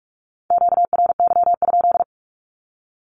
Morse code, heard as a keyed tone, about 44 wpm, 700 Hz. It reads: QRY?
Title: Q-Codes Repeated 3x 45wpm